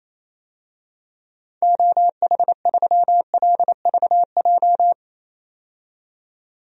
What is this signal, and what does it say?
Morse code, heard as a keyed tone, about 28 wpm, 700 Hz. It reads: OH3LVJ
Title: DXCC-100 Call Signs - 6 Characters - 28wpm